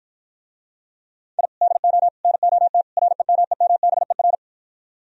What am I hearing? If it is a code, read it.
Morse code, heard as a keyed tone, about 53 wpm, 700 Hz. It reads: I DO NOT REMEMBER